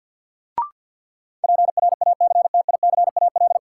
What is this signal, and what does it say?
Morse code, heard as a keyed tone, about 53 wpm, 700 Hz. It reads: PRACTICAL
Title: Sentences from Top 1400 Words - New Word Drill Alt - 50wpm